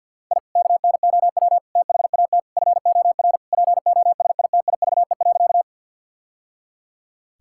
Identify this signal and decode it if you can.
Morse code, heard as a keyed tone, about 47 wpm, 700 Hz. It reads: I KNOW THAT FOR POSITIVE.
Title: Sentences from Top 1000 Words - Review Part 5 - 50wpm